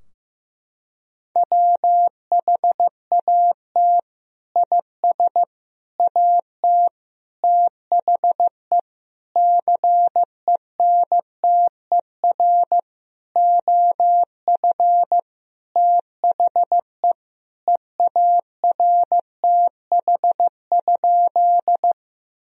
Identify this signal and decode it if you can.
Morse code, heard as a keyed tone, about 16 wpm, 700 Hz. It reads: WHAT IS AT THE CENTER OF THE EARTH?
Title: Sentences from Top 1000 Words - Review Part 2 - 15wpm